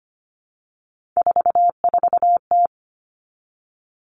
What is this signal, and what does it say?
Morse code, heard as a keyed tone, about 25 wpm, 700 Hz. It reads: 44T
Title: Three Challenging Letter-Number Combinations 25wpm